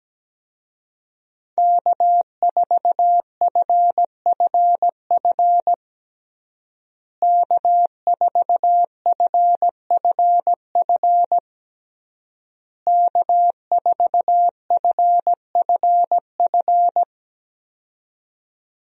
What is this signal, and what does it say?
Morse code, heard as a keyed tone, about 17 wpm, 700 Hz. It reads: K4FFF K4FFF K4FFF
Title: USA Call Signs - 5 Characters - Repeated 3x - 17wpm